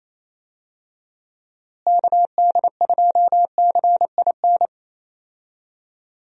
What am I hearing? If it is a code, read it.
Morse code, heard as a keyed tone, about 28 wpm, 700 Hz. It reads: KD2CIN
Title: USA Call Signs - 6 Characters - Repeated 3x - 28wpm